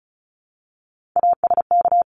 Morse code, heard as a keyed tone, about 34 wpm, 700 Hz. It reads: ASK
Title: Top 300 Words Repeated 35wpm